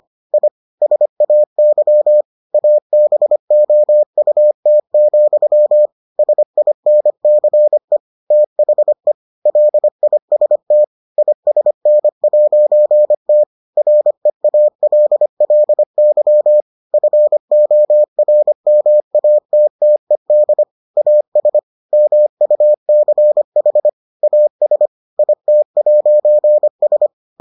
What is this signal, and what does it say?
Morse code, heard as a keyed tone, about 25 wpm, 600 Hz. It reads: I SAY ABOUT, SINCE THE LIST ISN'T REALLY FORMATTED AS MUCH AS IT'S